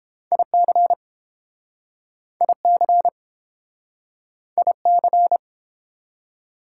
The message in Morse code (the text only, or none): IC IC IC